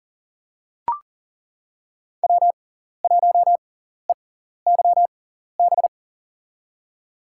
Morse code, heard as a keyed tone, about 37 wpm, spaced as for 14 wpm, 700 Hz. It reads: W1EYB
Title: USA Call Signs - 6x Character Spacing - 40wpm